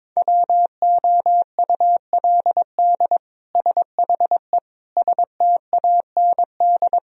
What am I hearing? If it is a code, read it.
Morse code, heard as a keyed tone, about 22 wpm, 700 Hz. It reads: WOULD SHE STAND
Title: Sentences from Top 1000 Words - Review Part 3 - 22wpm